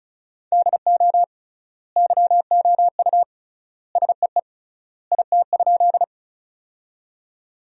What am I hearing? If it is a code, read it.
Morse code, heard as a keyed tone, about 36 wpm, 700 Hz. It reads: DO YOU SEE IT?
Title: 3x Word-Spacing - Sentences from Top 100 Words - 35wpm